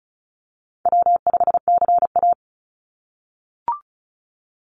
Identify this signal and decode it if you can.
Morse code, heard as a keyed tone, about 34 wpm, 700 Hz. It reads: W5CA
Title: USA Call Signs 35wpm